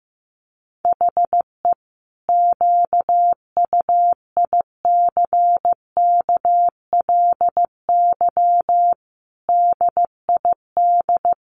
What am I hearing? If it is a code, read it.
Morse code, heard as a keyed tone, about 15 wpm, 700 Hz. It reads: HE QUICKLY DID
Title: Sentences from Top 700 Words 15wpm